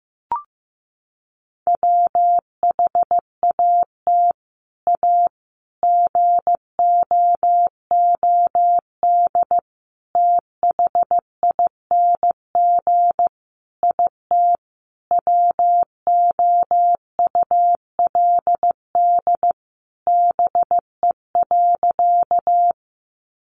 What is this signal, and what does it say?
Morse code, heard as a keyed tone, about 15 wpm, 700 Hz. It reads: WHAT A GOOD THING IT WOULD BE.
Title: Sentences from Top 200 Words 15wpm